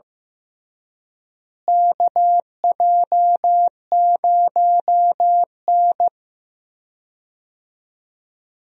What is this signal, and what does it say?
Morse code, heard as a keyed tone, about 15 wpm, 700 Hz. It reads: KJ0N
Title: USA Call Signs - 4 Characters - Repeated 3x - 15wpm